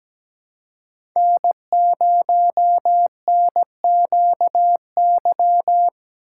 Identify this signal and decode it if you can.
Morse code, heard as a keyed tone, about 17 wpm, 700 Hz. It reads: N0NQY
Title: USA Call Signs - 5 Characters - 2x Character Spacing - 17wpm